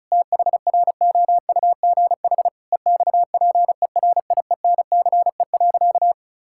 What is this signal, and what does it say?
Morse code, heard as a keyed tone, about 36 wpm, 700 Hz. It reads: THROUGH EXPERIENCE.
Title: Sentences from Top 1000 Words - Review Part 5 - 35wpm